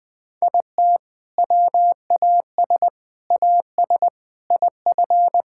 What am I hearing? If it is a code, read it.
Morse code, heard as a keyed tone, about 20 wpm, 700 Hz. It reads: IT WAS AS IF
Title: Sentences with 4-Word Phrases - Spoken Beginning - 20wpm